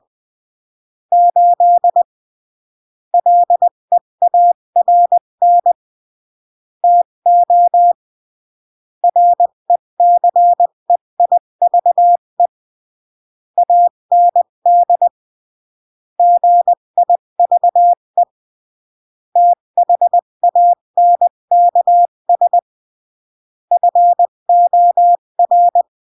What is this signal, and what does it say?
Morse code, heard as a keyed tone, about 20 wpm, 700 Hz. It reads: 8 LEARN TO RECEIVE AND GIVE THANKS FOR
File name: CW / jodorowsky / 82 commandments 20-15